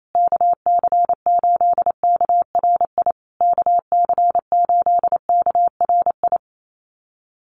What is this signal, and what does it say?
Morse code, heard as a keyed tone, about 28 wpm, 700 Hz. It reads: KC8KRI KC8KRI
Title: USA Call Signs Repeated 3x 28wpm